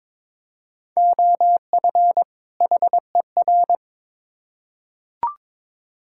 Morse code, heard as a keyed tone, about 22 wpm, 700 Hz. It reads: OF HER
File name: Build A Phrase - 2-Word Phrases - 22wpm